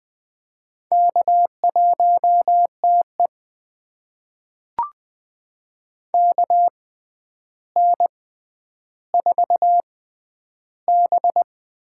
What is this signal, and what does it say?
Morse code, heard as a keyed tone, about 20 wpm, spaced as for 10 wpm, 700 Hz. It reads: K1TE KN4B
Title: USA Call Signs - 4 Characters - 6x Character Spacing - 20wpm